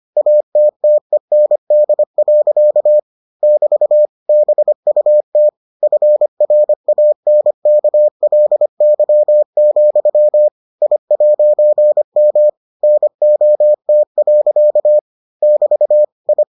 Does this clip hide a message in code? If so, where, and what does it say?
Yes: Morse code from 0.2 to 16.4 s, heard as a keyed tone, about 25 wpm, 600 Hz. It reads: ATTEND. <BT> BUT FRANKLY, I'M NOT. <BT> I